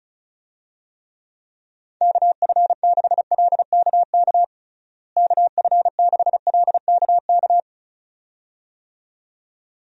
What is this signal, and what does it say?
Morse code, heard as a keyed tone, about 34 wpm, 700 Hz. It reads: KF6LKK KF6LKK
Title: USA Call Signs - 6 Characters - Repeated 2x - 35wpm